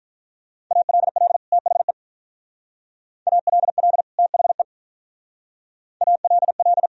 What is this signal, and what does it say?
Morse code, heard as a keyed tone, about 45 wpm, 700 Hz. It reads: ALL THE ALL THE ALL